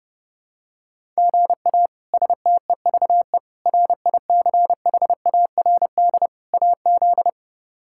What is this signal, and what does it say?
Morse code, heard as a keyed tone, about 30 wpm, 700 Hz. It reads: GA STEVE RICHARD AZ
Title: K1USN SST QSO Practice - 30wpm